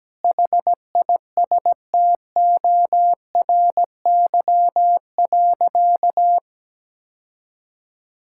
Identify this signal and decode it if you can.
Morse code, heard as a keyed tone, about 17 wpm, 700 Hz. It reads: HISTORY.
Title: Sentences from Top 1000 Words - Review Part 5 - 17wpm